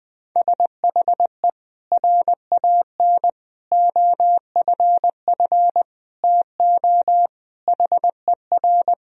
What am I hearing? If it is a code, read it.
Morse code, heard as a keyed tone, about 21 wpm, 700 Hz. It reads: SHE RAN OFF TO HER
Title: Sentences from Top 1000 Words - Review Part 3 - 20wpm